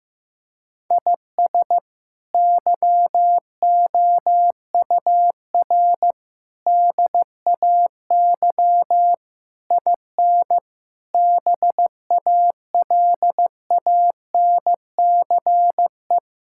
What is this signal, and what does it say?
Morse code, heard as a keyed tone, about 15 wpm, 700 Hz. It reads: IS YOUR DAY IN BALANCE
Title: Sentences from Top 1000 Words - Review Part 6 - 15wpm